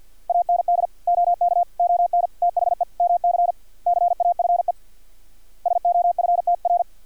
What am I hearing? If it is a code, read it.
Morse code, heard as a keyed tone, about 50 wpm, 700 Hz. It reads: AND OK ON THE MC CAFE SOFTR
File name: archived / CRSnet